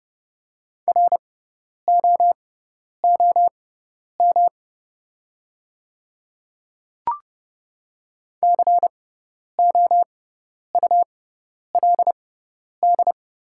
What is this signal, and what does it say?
Morse code, heard as a keyed tone, about 30 wpm, spaced as for 10 wpm, 700 Hz. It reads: ROOM COULD
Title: 6x Character Spacing - Top 300 Words - 30wpm